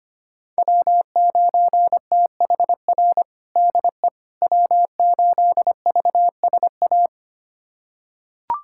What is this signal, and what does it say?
Morse code, heard as a keyed tone, about 25 wpm, 700 Hz. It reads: W9THR DE W8VSA
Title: USA Callsign DE Callsign - 5 Characters - 25wpm